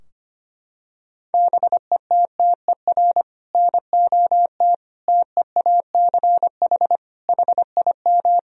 Morse code, heard as a keyed tone, about 25 wpm, 700 Hz. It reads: BETTER NOT TEACH HIM
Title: Sentences from Top 1000 Words - Review Part 2 - 25wpm